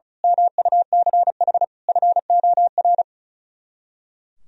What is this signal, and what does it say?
Morse code, heard as a keyed tone, about 33 wpm, 700 Hz. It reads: MUCH FOR
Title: Speed-Racing - Sets of 5 Words 35wpm